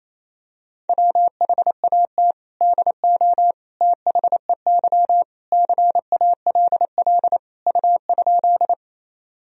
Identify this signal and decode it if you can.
Morse code, heard as a keyed tone, about 28 wpm, 700 Hz. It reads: WHAT DO THEY CALL U?
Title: Sentences from Top 100 Words and Abbreviations 28wpm